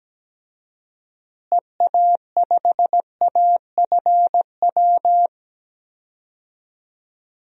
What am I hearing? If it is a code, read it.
Morse code, heard as a keyed tone, about 17 wpm, 700 Hz. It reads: EA5AFW
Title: DXCC-100 Call Signs - 6 Characters - Repeated 2x - 17wpm